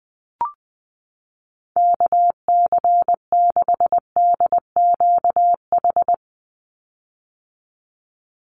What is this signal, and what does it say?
Morse code, heard as a keyed tone, about 20 wpm, 700 Hz. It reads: KC6DQH
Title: USA Call Signs 20wpm